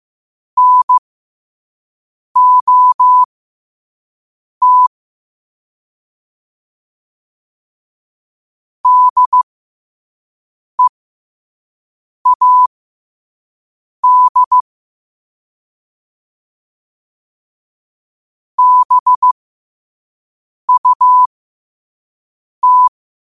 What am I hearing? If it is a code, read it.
Morse code, heard as a keyed tone, about 15 wpm, spaced as for 5 wpm, 1000 Hz. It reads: NOT DEAD BUT